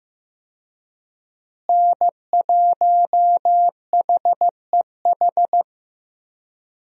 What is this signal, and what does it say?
Morse code, heard as a keyed tone, about 15 wpm, 700 Hz. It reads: N1HEH
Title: USA Call Signs - 5 Characters - Repeated 2x - 15wpm